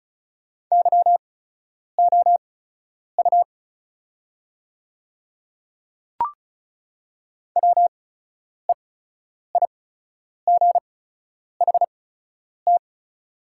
Morse code, heard as a keyed tone, about 36 wpm, spaced as for 10 wpm, 700 Hz. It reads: YOU WEIGHT
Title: 8x Character Spacing - Top 500 Words - 35wpm